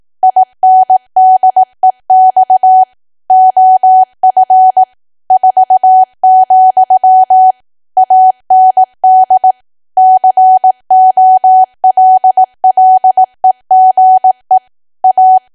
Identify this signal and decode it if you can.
Morse code, heard as a keyed tone, about 18 wpm, 750 Hz. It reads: INDEX OF 4, AND COLLEGE A
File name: cwa / W1AW-Level2 / 18WPM